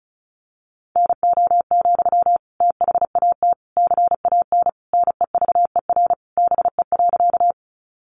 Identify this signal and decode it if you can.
Morse code, heard as a keyed tone, about 34 wpm, 700 Hz. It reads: NO, THAT CAN NEVER BE.
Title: Sentences from Top 300 Words 35wpm